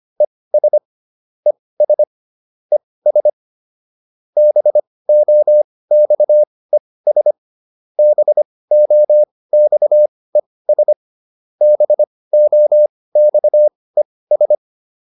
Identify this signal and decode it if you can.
Morse code, heard as a keyed tone, about 25 wpm, spaced as for 18 wpm, 600 Hz. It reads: ES ES ES BOXES BOXES BOXES